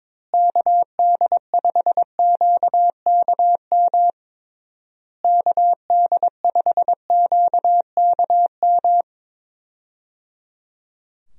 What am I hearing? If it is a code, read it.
Morse code, heard as a keyed tone, about 22 wpm, 700 Hz. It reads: KD5QKM KD5QKM
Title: USA Call Signs - 6 Characters - Repeated 3x - 22wpm